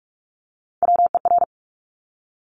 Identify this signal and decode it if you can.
Morse code, heard as a keyed tone, about 46 wpm, 700 Hz. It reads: WER
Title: Top Three Letter Combinations in Words 45wpm